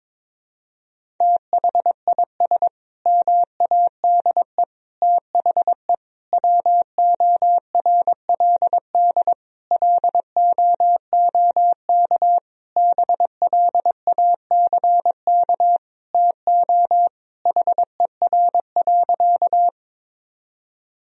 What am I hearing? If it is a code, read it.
Morse code, heard as a keyed tone, about 22 wpm, 700 Hz. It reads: THIS MADE THE WORLD LOOK BLACK TO HER.
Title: Sentences from Top 1000 Words - Review Part 2 - 22wpm